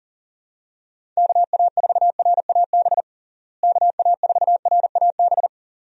Morse code, heard as a keyed tone, about 40 wpm, 700 Hz. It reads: KA4RAB KA4RAB
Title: USA Call Signs - 6 Characters - Repeated 2x - 40wpm